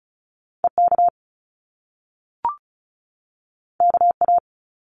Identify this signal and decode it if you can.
Morse code, heard as a keyed tone, about 36 wpm, 700 Hz. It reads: EK KA
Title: Top Two Letter Combinations in Words 35wpm